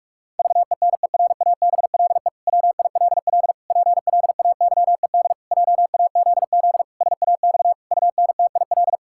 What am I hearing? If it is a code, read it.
Morse code, heard as a keyed tone, about 46 wpm, 700 Hz. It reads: VENERABLE WILL PLAYED JAZZ SAX UNTIL
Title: Pangram Challenge v2 - 45wpm